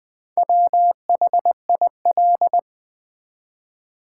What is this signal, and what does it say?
Morse code, heard as a keyed tone, about 20 wpm, 700 Hz. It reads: WHIL
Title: Build A Word - 1 Letter At A Time - 5-Letter Words - 20wpm